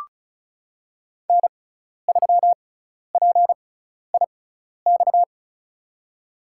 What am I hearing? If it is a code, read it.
Morse code, heard as a keyed tone, about 34 wpm, spaced as for 12 wpm, 700 Hz. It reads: N3PIX
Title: USA Call Signs - 5 Characters - 6x Character Spacing - 35wpm